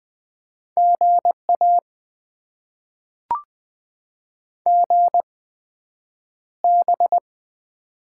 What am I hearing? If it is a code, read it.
Morse code, heard as a keyed tone, about 20 wpm, spaced as for 9 wpm, 700 Hz. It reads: GA GB
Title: Top 90 QSO Elements - 8x Character Spacing - 20wpm